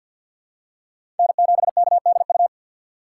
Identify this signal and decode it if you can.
Morse code, heard as a keyed tone, about 48 wpm, 690 Hz. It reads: N7KDU
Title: USA Call Signs - 5 Characters - 2x Character Spacing - 50wpm